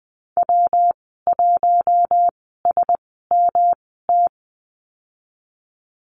Morse code, heard as a keyed tone, about 20 wpm, spaced as for 15 wpm, 700 Hz. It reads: W1SMT
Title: 147 Lesson 24g - Letter y - Call Signs 20wpm